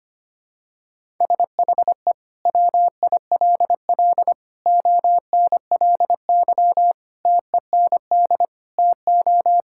Morse code, heard as a keyed tone, about 25 wpm, 700 Hz. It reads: SHE WILL ONLY TEND TO
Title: Sentences from Top 1000 Words - Review Part 5 - 25wpm